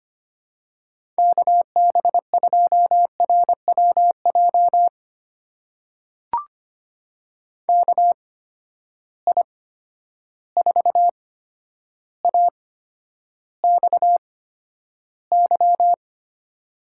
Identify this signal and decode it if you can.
Morse code, heard as a keyed tone, about 25 wpm, spaced as for 11 wpm, 700 Hz. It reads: KB2RWJ KI4AXY
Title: USA Call Signs - 6 Characters - 8x Character Spacing - 25wpm